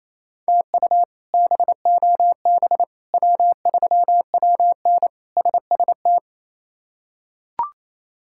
Morse code, heard as a keyed tone, about 28 wpm, 700 Hz. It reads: TU BOB W3WN SST
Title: K1USN SST QSO Practice - 28wpm